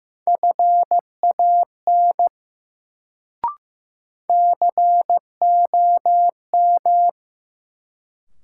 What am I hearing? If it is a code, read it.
Morse code, heard as a keyed tone, about 15 wpm, 700 Hz. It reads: FAN COM
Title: Sentences from Top 1300 Words - New Word Drill Alt - 15wpm